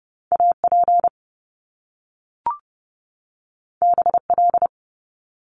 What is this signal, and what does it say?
Morse code, heard as a keyed tone, about 30 wpm, 700 Hz. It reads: AP BL
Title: Top Two Letter Combinations in Words 30wpm